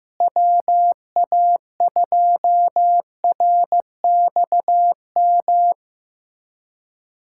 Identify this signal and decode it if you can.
Morse code, heard as a keyed tone, about 15 wpm, 700 Hz. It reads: WA2RXM
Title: USA Callsign DE Callsign 15wpm